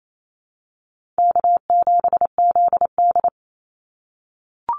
Morse code, heard as a keyed tone, about 28 wpm, 700 Hz. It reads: K7ZD
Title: Easy Call Signs - 28wpm